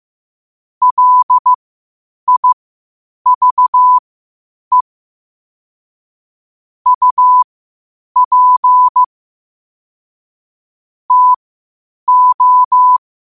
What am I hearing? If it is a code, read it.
Morse code, heard as a keyed tone, about 15 wpm, spaced as for 8 wpm, 1000 Hz. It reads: LIVE UP TO
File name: qotdfiles / 8wpm